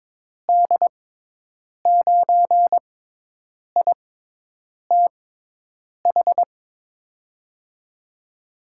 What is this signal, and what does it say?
Morse code, heard as a keyed tone, about 22 wpm, spaced as for 8 wpm, 700 Hz. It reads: D9ITH
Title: USA Call Signs - 6 Characters - 6x Character Spacing - 22wpm